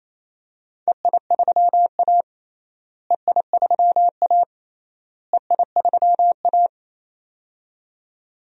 Morse code, heard as a keyed tone, about 28 wpm, 700 Hz. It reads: EI3A EI3A EI3A
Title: DXCC-100 Call Signs - 4 Characters - Repeated 3x - 28wpm